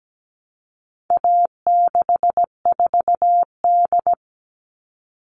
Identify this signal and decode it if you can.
Morse code, heard as a keyed tone, about 17 wpm, 700 Hz. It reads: A64D
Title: All Cut Numbers - 4-Digit 17wpm